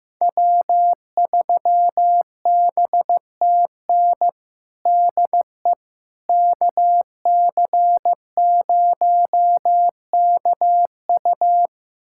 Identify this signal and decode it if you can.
Morse code, heard as a keyed tone, about 15 wpm, 700 Hz. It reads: W3BTN DE KC0KU
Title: USA Callsign DE Callsign - 5 Characters - 15wpm